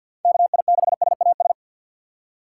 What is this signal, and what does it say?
Morse code, heard as a keyed tone, about 46 wpm, 700 Hz. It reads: KI6SAS
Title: USA Call Signs - 3x Character Spacing - 50wpm